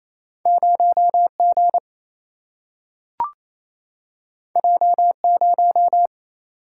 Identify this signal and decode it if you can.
Morse code, heard as a keyed tone, about 28 wpm, 700 Hz. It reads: 0 G J 0
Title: Single-Letter Single-Number 28wpm